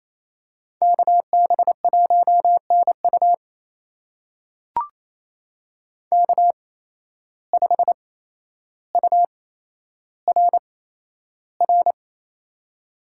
Morse code, heard as a keyed tone, about 29 wpm, spaced as for 12 wpm, 700 Hz. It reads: KB1NU K5URR
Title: USA Call Signs - 5 Characters - 8x Character Spacing - 28wpm